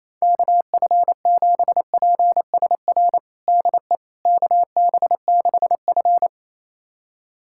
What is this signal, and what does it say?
Morse code, heard as a keyed tone, about 28 wpm, 700 Hz. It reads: KF7PSR DE KB6F
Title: USA Callsign DE Callsign 28wpm